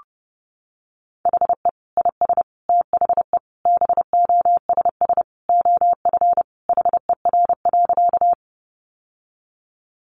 Morse code, heard as a keyed tone, about 30 wpm, 700 Hz. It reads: HE IS THE BOSS OF HER.
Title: Sentences from Top 700 Words 30wpm